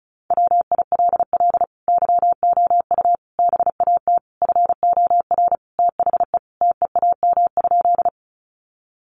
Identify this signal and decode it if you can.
Morse code, heard as a keyed tone, about 34 wpm, 700 Hz. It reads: WILL YOU BAT FOR THE TEAM?
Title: Sentences from Top 1500 Words 35wpm